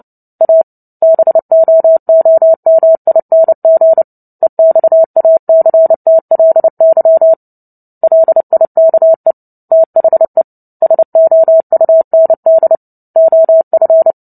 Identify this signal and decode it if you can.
Morse code, heard as a keyed tone, about 29 wpm, 650 Hz. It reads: A BOOMING EXACTLY LIKE THE SOUND OF
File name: Book Two / 03-Chapter 3 - 29 WPM